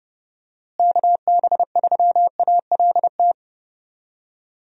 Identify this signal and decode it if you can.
Morse code, heard as a keyed tone, about 30 wpm, 700 Hz. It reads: KB3ALT
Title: USA Call Signs - 6 Characters - Repeated 3x - 30wpm